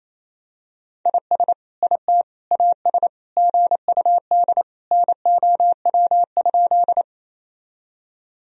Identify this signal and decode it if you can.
Morse code, heard as a keyed tone, about 28 wpm, 700 Hz. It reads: IS IT AS GUD NOW?
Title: Sentences from Top 200 Words and Abbreviations 28wpm